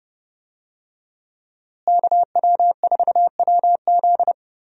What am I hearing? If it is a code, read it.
Morse code, heard as a keyed tone, about 32 wpm, 700 Hz. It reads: KW4WZ
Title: USA Call Signs - 5 Characters - Repeated 2x - 30wpm